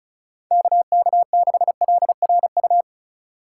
Morse code, heard as a keyed tone, about 36 wpm, 700 Hz. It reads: KK6LRU
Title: USA Call Signs - 6 Characters - 3x Character Spacing - 35wpm